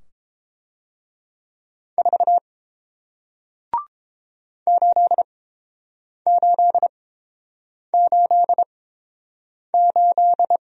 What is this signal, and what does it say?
Morse code, heard as a keyed tone, about 32 wpm, spaced as for 8 wpm, 700 Hz. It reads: 4 8888